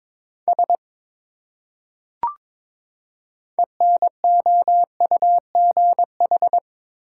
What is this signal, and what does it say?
Morse code, heard as a keyed tone, about 22 wpm, 700 Hz. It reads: S ENOUGH
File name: Build A Phrase - 2-Word Phrases - 22wpm